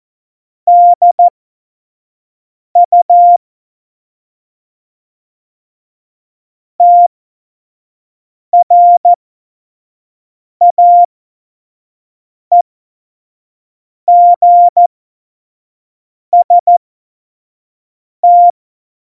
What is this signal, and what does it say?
Morse code, heard as a keyed tone, about 14 wpm, spaced as for 5 wpm, 700 Hz. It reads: DU TRAEGST
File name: Morsekurs / Witze / Witz 25. Juni 2019, 25 F70